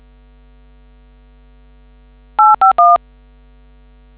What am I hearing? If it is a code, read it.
DTMF, heard as telephone touch tones, tones of about 180 ms. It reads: 851